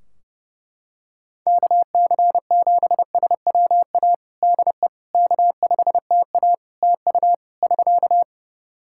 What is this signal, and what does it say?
Morse code, heard as a keyed tone, about 30 wpm, 700 Hz. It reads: KC7SWA DE K5TA TU <SK>
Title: Practice QSO - Exchange 4 - Ending - 30wpm